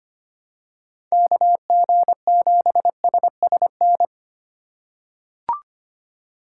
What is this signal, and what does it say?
Morse code, heard as a keyed tone, about 25 wpm, 700 Hz. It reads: KG7SSN
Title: USA Call Signs - 6 Characters - 2x Character Spacing - 25wpm